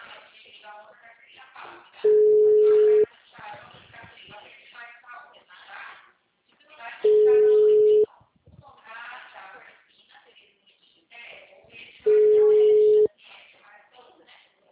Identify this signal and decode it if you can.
Ringback tone (Europe, 425 Hz), with speech.